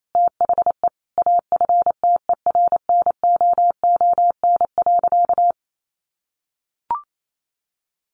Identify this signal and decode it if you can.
Morse code, heard as a keyed tone, about 28 wpm, 700 Hz. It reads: THE AFTERNOON.
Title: Sentences from Top 1500 Words 28wpm